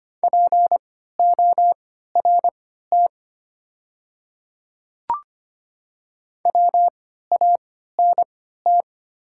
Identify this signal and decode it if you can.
Morse code, heard as a keyed tone, about 25 wpm, spaced as for 14 wpm, 690 Hz. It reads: PORT WANT
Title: 3x Character Spacing - Top 300 Words - 25wpm